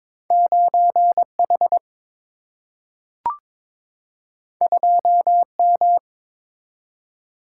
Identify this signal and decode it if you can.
Morse code, heard as a keyed tone, about 22 wpm, 700 Hz. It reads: 9 H 2 M
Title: Single-Letter Single-Number 22wpm